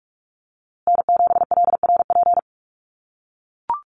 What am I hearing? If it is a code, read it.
Morse code, heard as a keyed tone, about 44 wpm, 700 Hz. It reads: N7LRP
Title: USA Call Signs Repeated 2x 45wpm